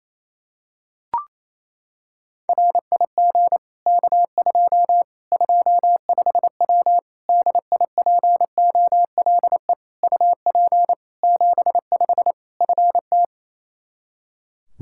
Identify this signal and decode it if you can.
Morse code, heard as a keyed tone, about 28 wpm, 700 Hz. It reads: RIG K2 25W DIPOLE UP 75 FT
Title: Practice QSO - Exchange 2 - Info - 28wpm